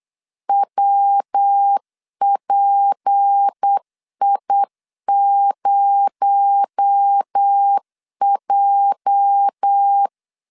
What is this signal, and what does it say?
Morse code, heard as a keyed tone, about 8 wpm, 800 Hz. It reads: WPI0J
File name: LEZIONI / 10 SIGNS / 10A SLASH 8wpm 3452